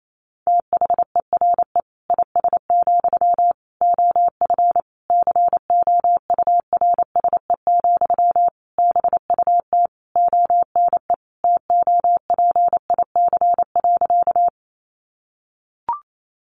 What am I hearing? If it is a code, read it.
Morse code, heard as a keyed tone, about 28 wpm, 700 Hz. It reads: THERE IS, OF COURSE, BUT ONE TOPIC.
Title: Sentences from Top 1200 Words 28wpm